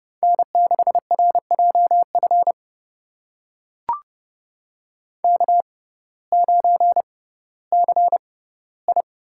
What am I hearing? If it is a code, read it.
Morse code, heard as a keyed tone, about 29 wpm, spaced as for 16 wpm, 700 Hz. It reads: N6RJF K9CI
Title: USA Call Signs - 5 Characters - 6x Character Spacing - 30wpm